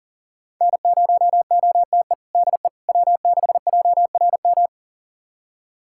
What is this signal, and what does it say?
Morse code, heard as a keyed tone, about 40 wpm, 700 Hz. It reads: N0OTE DE W6JRM
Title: USA Callsign DE Callsign - 5 Characters - 40wpm